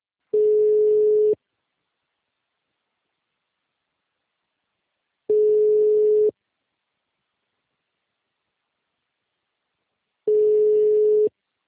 Ringback tone (Europe, 425 Hz).